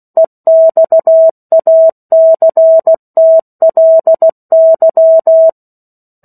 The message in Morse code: EXACTLY

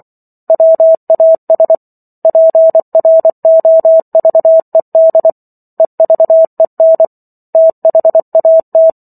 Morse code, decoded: WAS PROVED EVEN THAT

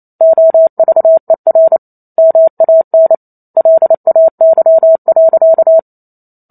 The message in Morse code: OVER MAN LAY.

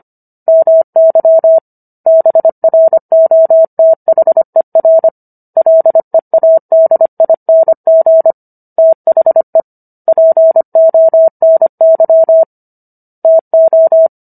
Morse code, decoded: MY BROTHER LEADING THE PONY TO

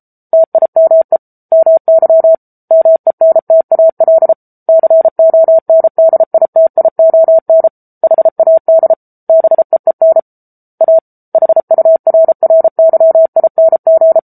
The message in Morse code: TIME MY MENTAL CONDITION HAD BEEN A HURRYING

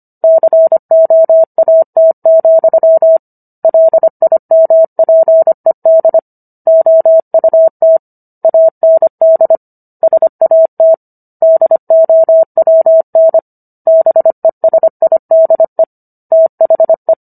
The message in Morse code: COAT, LIMPED OUT AND SAT DOWN BESIDE THE

